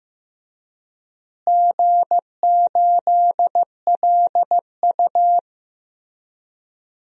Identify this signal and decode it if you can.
Morse code, heard as a keyed tone, about 16 wpm, 700 Hz. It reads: G8LU
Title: DXCC-100 Call Signs - 4 Characters - Repeated 3x - 15wpm